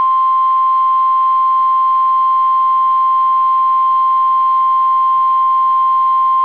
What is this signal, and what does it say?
Emergency Alert System (SAME) tones (the 1050 Hz attention tone). It is heard from 0.4 to 6.0 s.